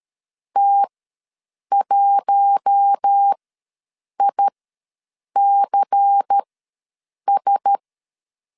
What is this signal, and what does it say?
Morse code, heard as a keyed tone, about 13 wpm, spaced as for 7 wpm, 800 Hz. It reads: T1ICS
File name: LEZIONI / 5 LBVC / 5 LBVC ws 12wpm 3159